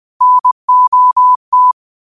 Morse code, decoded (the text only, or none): NOT